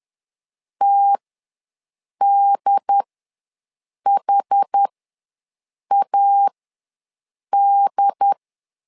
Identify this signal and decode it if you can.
Morse code, heard as a keyed tone, about 10 wpm, spaced as for 6 wpm, 800 Hz. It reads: TDHAD
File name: LEZIONI / 2 UHDN / 2 UHD ws 10wpm 2058